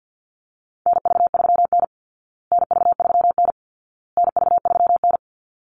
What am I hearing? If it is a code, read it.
Morse code, heard as a keyed tone, about 50 wpm, 700 Hz. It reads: NV3N NV3N NV3N
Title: Call Signs Repeated 2x - 50wpm